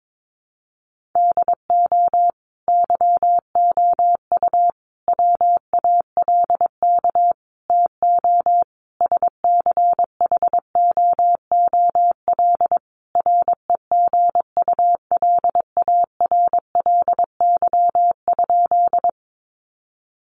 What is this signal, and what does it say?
Morse code, heard as a keyed tone, about 22 wpm, 700 Hz. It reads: DO YOU WALK TO SCHOOL REGULARLY?
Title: Sentences from Top 1400 Words 22wpm